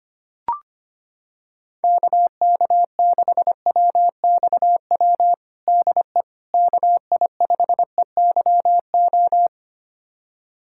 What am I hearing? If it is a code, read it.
Morse code, heard as a keyed tone, about 25 wpm, 700 Hz. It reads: KK6WXW DE KI5EYO
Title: USA Callsign DE Callsign - 6 Characters - 25wpm